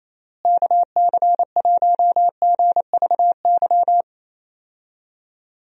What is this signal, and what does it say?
Morse code, heard as a keyed tone, about 28 wpm, 700 Hz. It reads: KC1GVY
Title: USA Call Signs - 6 Characters - Repeated 2x - 28wpm